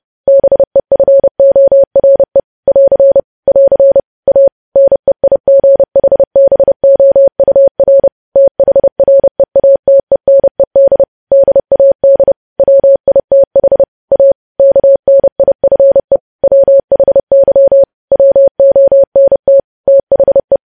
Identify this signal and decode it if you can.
Morse code, heard as a keyed tone, about 30 wpm, 550 Hz. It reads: BEFORE <AR> <AR> A NEIGHBOUR THREATENED DAD WITH A KNIFE WHY WONT THE
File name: podcast / News headlines-1x-30x30